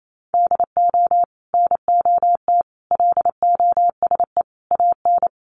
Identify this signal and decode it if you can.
Morse code, heard as a keyed tone, about 28 wpm, 700 Hz. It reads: DO NOT LOSE AN
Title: Sentences from Top 1100 Words 28wpm